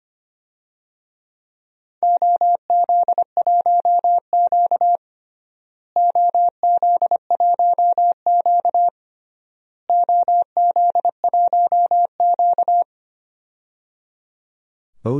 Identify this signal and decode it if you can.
Morse code, heard as a keyed tone, about 25 wpm, 700 Hz. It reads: OZ1Q OZ1Q OZ1Q
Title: DXCC-100 Call Signs - 4 Characters - Repeated 3x - 25wpm